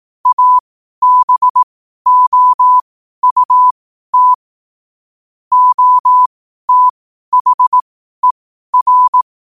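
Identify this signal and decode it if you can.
Morse code, heard as a keyed tone, about 18 wpm, spaced as for 12 wpm, 1000 Hz. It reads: ABOUT OTHER
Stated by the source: Morse code quote-of-the-day practice at 12 words per minute.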